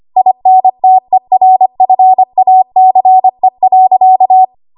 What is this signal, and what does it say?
Morse code, heard as a keyed tone, about 25 wpm, 750 Hz. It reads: INTERFACE.
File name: Archive / 25 WPM / 25WPM